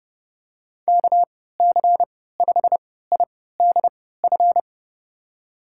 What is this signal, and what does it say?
Morse code, heard as a keyed tone, about 30 wpm, spaced as for 17 wpm, 700 Hz. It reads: KC5IDF
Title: USA Call Signs - 3x Character Spacing - 30wpm